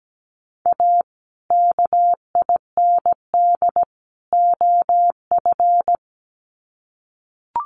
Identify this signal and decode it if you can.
Morse code, heard as a keyed tone, about 17 wpm, 700 Hz. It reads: A KIND OF
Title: Sets of 3 Words 17wpm